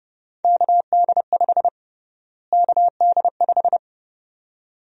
Morse code, heard as a keyed tone, about 32 wpm, 700 Hz. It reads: KD5 KD5
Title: USA Call Sign Prefixes - Repeated 2x - 30wpm